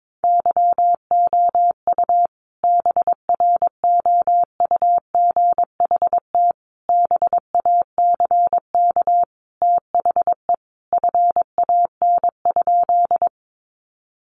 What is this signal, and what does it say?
Morse code, heard as a keyed tone, about 22 wpm, 700 Hz. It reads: YOU BROUGHT BACK THE FAN?
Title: Sentences from Top 1300 Words 22wpm